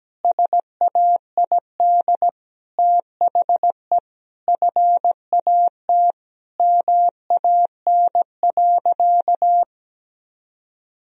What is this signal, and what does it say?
Morse code, heard as a keyed tone, about 17 wpm, 700 Hz. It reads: SAID THE FAT MAN.
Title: Sentences from Top 1000 Words - Review Part 4 - 17wpm